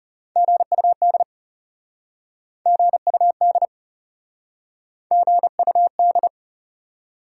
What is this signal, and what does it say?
Morse code, heard as a keyed tone, about 40 wpm, 700 Hz. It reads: GUD GUD GUD